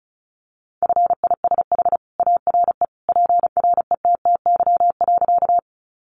Morse code, heard as a keyed tone, about 34 wpm, 700 Hz. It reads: FISH ARE PRETTY.
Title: Easy Phrases 35wpm